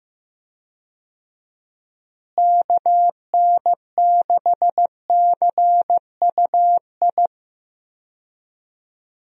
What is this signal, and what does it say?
Morse code, heard as a keyed tone, about 16 wpm, 700 Hz. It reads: KN6CUI
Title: USA Call Signs - 6 Characters - Repeated 3x - 15wpm